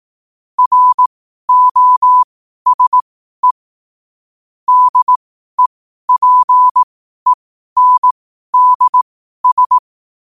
Morse code, heard as a keyed tone, about 18 wpm, spaced as for 12 wpm, 1000 Hz. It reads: ROSE DEPENDS
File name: qotdfiles / 12wpm